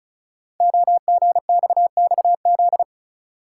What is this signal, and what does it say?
Morse code, heard as a keyed tone, about 35 wpm, 700 Hz. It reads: OGXXZ
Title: Code Groups - 5 Letter:Numbers - 35wpm